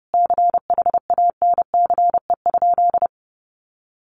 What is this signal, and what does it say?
Morse code, heard as a keyed tone, about 30 wpm, 700 Hz. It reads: CHANCE?
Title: Sentences from Top 800 Words 30wpm